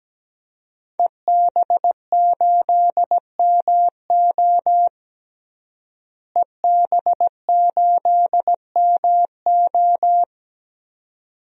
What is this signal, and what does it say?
Morse code, heard as a keyed tone, about 17 wpm, 700 Hz. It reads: EB8MO EB8MO
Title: DXCC-100 Call Signs - 5 Characters - Repeated 2x - 17wpm